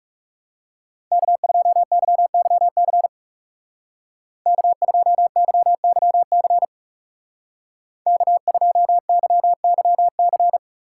Morse code, heard as a keyed tone, about 35 wpm, 700 Hz. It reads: K2YYC K2YYC K2YYC
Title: Speed Racing - USA 1x3 Call Signs 30wpm